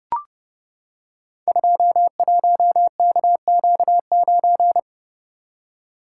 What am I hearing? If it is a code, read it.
Morse code, heard as a keyed tone, about 30 wpm, 700 Hz. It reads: 21KQ9
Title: Code Groups - 5 Letter:Numbers - 30wpm